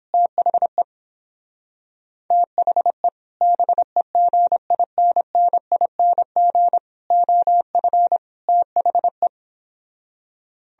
Morse code, heard as a keyed tone, about 29 wpm, 700 Hz. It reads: THE THE BEGINNING OF THE